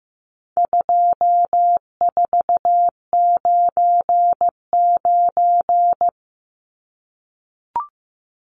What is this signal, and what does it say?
Morse code, heard as a keyed tone, about 15 wpm, 700 Hz. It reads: 2499
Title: Numbers - 4-Digit 15wpm